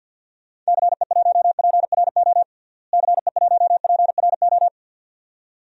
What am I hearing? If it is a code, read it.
Morse code, heard as a keyed tone, about 47 wpm, 700 Hz. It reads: CE1PRO CE1PRO
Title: DXCC-100 Call Signs - 6 Characters - Repeated 2x - 50wpm